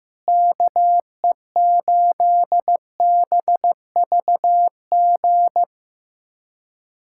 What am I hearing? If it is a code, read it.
Morse code, heard as a keyed tone, about 15 wpm, 700 Hz. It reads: KE8BVG
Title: USA Callsign DE Callsign - 6 Characters - 15wpm